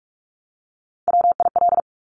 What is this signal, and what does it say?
Morse code, heard as a keyed tone, about 46 wpm, 700 Hz. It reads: WIL
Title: Top Three Letter Combinations in Words 45wpm